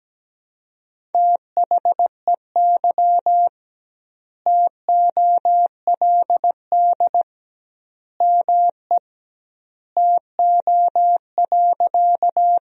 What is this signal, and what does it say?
Morse code, heard as a keyed tone, about 17 wpm, 700 Hz. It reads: THEY TOLD ME TO.
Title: 2x Word-Spacing - Sentences from Top 400 Words - 17wpm